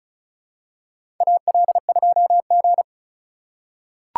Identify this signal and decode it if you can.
Morse code, heard as a keyed tone, about 36 wpm, 700 Hz. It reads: AL2G
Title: USA Call Signs - 4 Characters - 3x Character Spacing - 35wpm